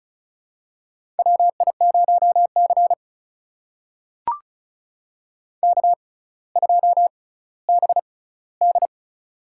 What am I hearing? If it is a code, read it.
Morse code, heard as a keyed tone, about 36 wpm, spaced as for 18 wpm, 700 Hz. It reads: WI0C K2BD
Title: USA Call Signs - 4 Characters - 6x Character Spacing - 35wpm